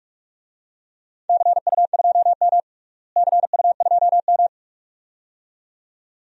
Morse code, heard as a keyed tone, about 46 wpm, 700 Hz. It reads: CU2M CU2M
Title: DXCC-100 Call Signs - 4 Characters - Repeated 2x - 45wpm